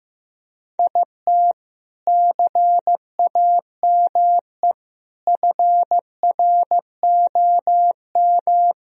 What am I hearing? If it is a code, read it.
Morse code, heard as a keyed tone, about 15 wpm, 700 Hz. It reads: IT CAME FROM
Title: Sentences from Top 1000 Words - Review Part 2 - 15wpm